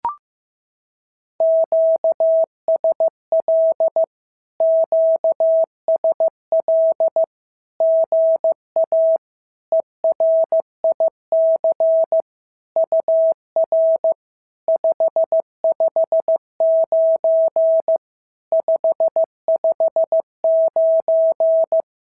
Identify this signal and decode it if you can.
Morse code, heard as a keyed tone, about 15 wpm, 650 Hz. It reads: QSL QSL GA ERIC UR 559 559
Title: SOTA QSO Practice 15wpm